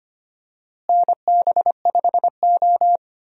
Morse code, heard as a keyed tone, about 25 wpm, 700 Hz. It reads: NB5O
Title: USA Call Signs - 4 Characters - Repeated 3x - 25wpm